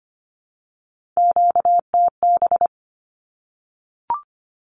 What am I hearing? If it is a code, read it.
Morse code, heard as a keyed tone, about 25 wpm, 700 Hz. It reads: QTB
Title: Q-Codes Repeated 3x 25wpm